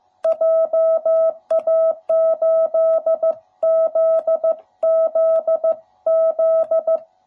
Morse code, heard as a keyed tone, about 14 wpm, 660 Hz. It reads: JA8ZZZ
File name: jh8cbh / 2021cw / cw07b